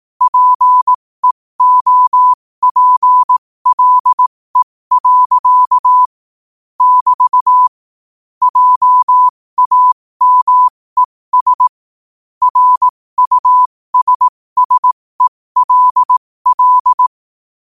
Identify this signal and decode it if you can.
Morse code, heard as a keyed tone, about 18 wpm, spaced as for 15 wpm, 1000 Hz. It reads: PEOPLE. <BT> JAMES RUSSELL
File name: qotdfiles / 15wpm